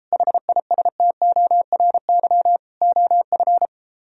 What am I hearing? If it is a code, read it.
Morse code, heard as a keyed tone, about 33 wpm, 700 Hz. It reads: HISTORY OF